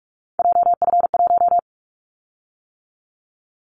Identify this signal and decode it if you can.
Morse code, heard as a keyed tone, about 46 wpm, 700 Hz. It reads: JF1
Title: Callsign Prefixes for Top 25 DXCC - 45wpm